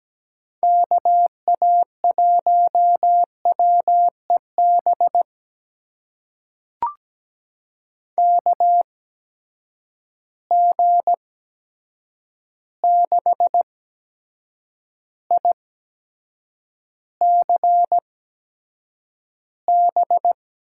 Morse code, heard as a keyed tone, about 17 wpm, spaced as for 7 wpm, 700 Hz. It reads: KA1WEB KG6ICB
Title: USA Call Signs - 6 Characters - 8x Character Spacing - 17wpm